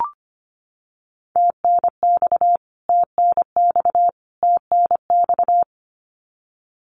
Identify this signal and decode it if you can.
Morse code, heard as a keyed tone, about 25 wpm, 700 Hz. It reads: TNX TNX TNX
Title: Abbreviations Repeated 3x - 25wpm